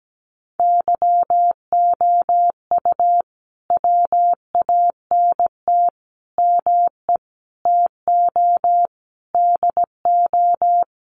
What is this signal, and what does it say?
Morse code, heard as a keyed tone, about 17 wpm, 700 Hz. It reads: YOU WANT ME TO DO
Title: Sets of 5 Words Encore2 17wpm